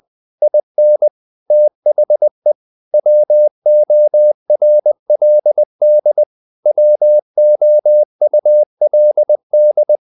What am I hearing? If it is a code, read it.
Morse code, heard as a keyed tone, about 20 wpm, 600 Hz. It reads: IN THE WORLD WOULD
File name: CW / Practice / EnglishQuotations0320